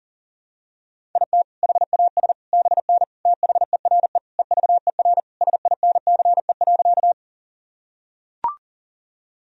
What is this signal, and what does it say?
Morse code, heard as a keyed tone, about 40 wpm, 700 Hz. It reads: IT HAS BN THERE EVER SINCE.
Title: Sentences from Top 400 Words and Abbreviations 40wpm